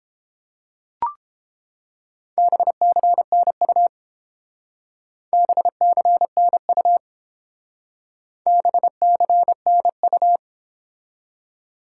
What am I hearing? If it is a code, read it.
Morse code, heard as a keyed tone, about 28 wpm, 700 Hz. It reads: BCNU BCNU BCNU